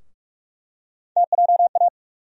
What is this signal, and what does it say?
Morse code, heard as a keyed tone, about 45 wpm, 700 Hz. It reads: TJA
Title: USA Call Sign Postfixes - 3 Letters 45wpm